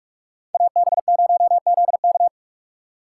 Morse code, heard as a keyed tone, about 44 wpm, 700 Hz. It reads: AB0ZK